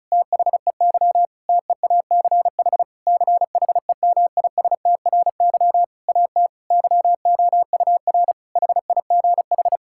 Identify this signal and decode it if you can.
Morse code, heard as a keyed tone, about 34 wpm, 700 Hz. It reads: THEY TEACH CHEMISTRY AT YOUR HIGH
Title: Sentences from Top 1000 Words - Review Part 6 - 35wpm